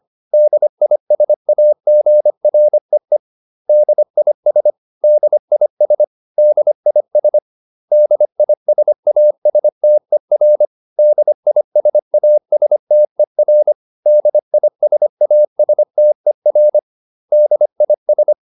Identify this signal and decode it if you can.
Morse code, heard as a keyed tone, about 25 wpm, 600 Hz. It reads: DISAGREE DIS DIS DIS DISASTER DISASTER DISASTER DIS
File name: DIS8 25